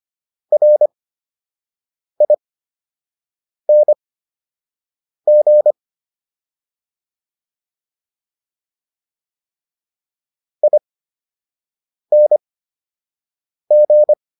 Morse code, heard as a keyed tone, about 25 wpm, spaced as for 6 wpm, 600 Hz. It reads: RING ING